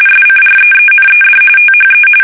RTTY, about 45.45 baud. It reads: SCREENED THE
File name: fsk850 / battleshipnj